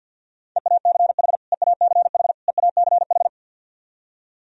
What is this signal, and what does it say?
Morse code, heard as a keyed tone, about 49 wpm, 700 Hz. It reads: EACH EACH EACH
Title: Top 100 Words Repeated 50wpm